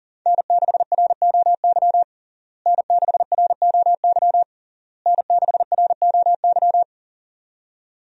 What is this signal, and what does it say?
Morse code, heard as a keyed tone, about 40 wpm, 700 Hz. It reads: N6ROY N6ROY N6ROY
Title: USA Call Signs - 5 Characters - Repeated 3x - 40wpm